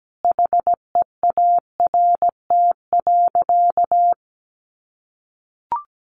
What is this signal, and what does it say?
Morse code, heard as a keyed tone, about 17 wpm, 700 Hz. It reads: HEART.
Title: Sentences from Top 1200 Words 17wpm